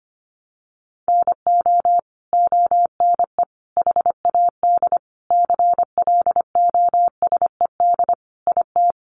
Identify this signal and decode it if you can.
Morse code, heard as a keyed tone, about 25 wpm, 700 Hz. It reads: NO ONE HAD CLOSED IT
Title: Sentences from Top 1400 Words 25wpm